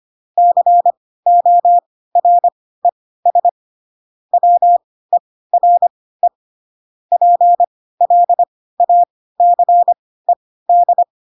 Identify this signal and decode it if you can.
Morse code, heard as a keyed tone, about 25 wpm, spaced as for 16 wpm, 700 Hz. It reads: CORES WERE PLACED
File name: Pinocchio / Pinocchio ch7 10 25 ik1hge